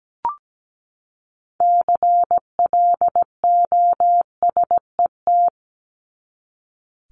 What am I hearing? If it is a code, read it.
Morse code, heard as a keyed tone, about 17 wpm, 700 Hz. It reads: CLOSET
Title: Top 1751-2000 Words 17wpm